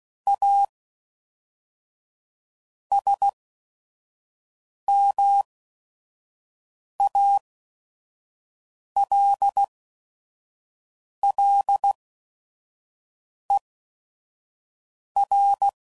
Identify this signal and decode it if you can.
Morse code, heard as a keyed tone, about 16 wpm, spaced as for 5 wpm, 800 Hz. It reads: A SMALLER
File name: CW Code Groups @5WPM